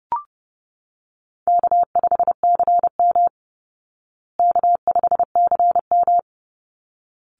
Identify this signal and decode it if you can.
Morse code, heard as a keyed tone, about 30 wpm, 700 Hz. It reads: K5CM K5CM
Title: Call Signs Repeated 3x - 30wpm